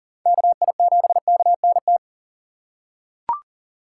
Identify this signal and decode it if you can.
Morse code, heard as a keyed tone, about 40 wpm, 700 Hz. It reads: KI7KNT
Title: USA Call Signs - 6 Characters - 2x Character Spacing - 40wpm